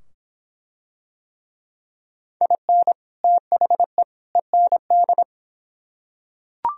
Morse code, heard as a keyed tone, about 23 wpm, 700 Hz. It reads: IN THE END